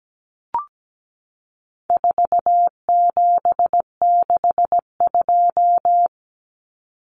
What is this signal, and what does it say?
Morse code, heard as a keyed tone, about 17 wpm, 700 Hz. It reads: 4762
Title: Numbers - 4-Digit 17wpm